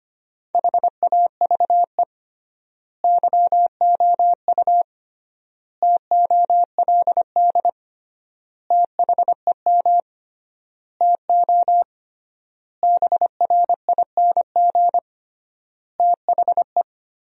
Morse code, heard as a keyed tone, about 25 wpm, 700 Hz. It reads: HAVE YOU TOLD THEM TO BRING THE
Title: 3x Word-Spacing - Sentences from Top 500 Words - 25wpm